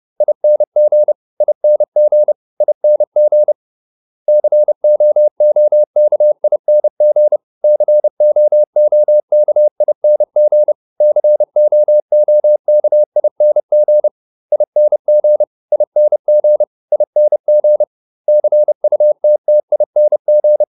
Morse code, heard as a keyed tone, about 30 wpm, 600 Hz. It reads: ING ING ING COOKING COOKING COOKING ING ING ING CUTTING